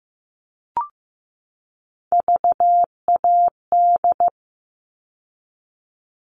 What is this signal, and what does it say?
Morse code, heard as a keyed tone, about 15 wpm, 700 Hz. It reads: VAD
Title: All Cut Numbers - 3-Digit 15wpm